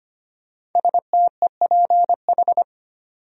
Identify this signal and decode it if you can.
Morse code, heard as a keyed tone, about 25 wpm, 700 Hz. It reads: STEPH
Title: USA Common Names - Max 5 Letters - 25wpm